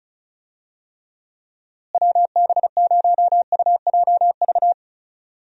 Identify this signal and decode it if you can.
Morse code, heard as a keyed tone, about 36 wpm, 700 Hz. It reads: WB0UJV
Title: USA Call Signs - 6 Characters - Repeated 2x - 35wpm